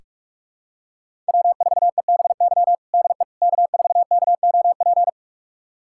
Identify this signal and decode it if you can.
Morse code, heard as a keyed tone, about 42 wpm, 700 Hz. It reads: W4EBY DE K4KOP
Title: USA Callsign DE Callsign - 5 Characters - 45wpm